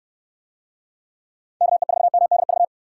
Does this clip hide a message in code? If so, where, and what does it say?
Yes: Morse code from 1.6 to 2.6 s, heard as a keyed tone, about 64 wpm, 700 Hz. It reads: KE4MDV